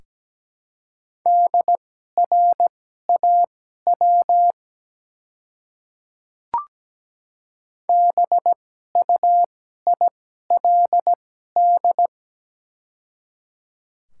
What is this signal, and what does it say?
Morse code, heard as a keyed tone, about 17 wpm, spaced as for 12 wpm, 700 Hz. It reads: DRAW BUILD
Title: 2x Character Spacing - Top 300 Words - 17wpm